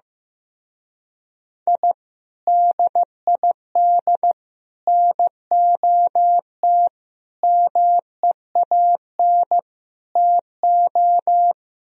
Morse code, heard as a keyed tone, about 15 wpm, 700 Hz. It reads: I DID NOT MEAN TO
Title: Speed-Racing - Sets of 5 Words 15wpm